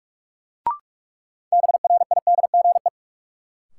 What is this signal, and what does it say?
Morse code, heard as a keyed tone, about 46 wpm, 700 Hz. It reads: BRIDGE
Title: Sentences from Top 1200 Words - New Word Drill - 45wpm